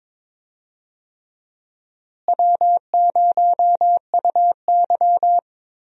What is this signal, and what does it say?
Morse code, heard as a keyed tone, about 22 wpm, 700 Hz. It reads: W0UY
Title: USA Call Signs - 4 Characters - Repeated 3x - 22wpm